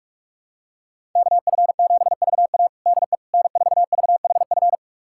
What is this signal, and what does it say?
Morse code, heard as a keyed tone, about 46 wpm, 700 Hz. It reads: KF7VA DE N4VHF
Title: USA Callsign DE Callsign - 5 Characters - 45wpm